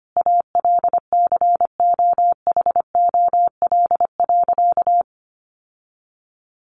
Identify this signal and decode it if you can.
Morse code, heard as a keyed tone, about 25 wpm, 700 Hz. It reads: ALCOHOL.
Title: Sentences from Top 1400 Words 25wpm